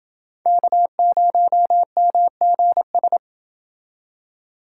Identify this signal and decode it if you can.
Morse code, heard as a keyed tone, about 28 wpm, 700 Hz. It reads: K0MGS